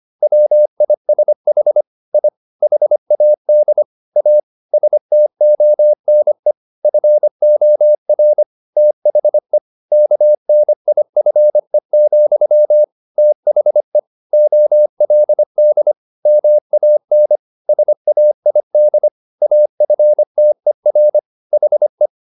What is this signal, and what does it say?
Morse code, heard as a keyed tone, about 25 wpm, 600 Hz. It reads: WISH I HAD A STONE FOR THE KNIFE, THE OLD MAN SAID AFTER HE